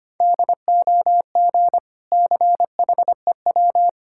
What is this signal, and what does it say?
Morse code, heard as a keyed tone, about 25 wpm, 700 Hz. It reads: DOG CHEW
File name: practice / 25wpm / Difficult Words with Hints - 25wpm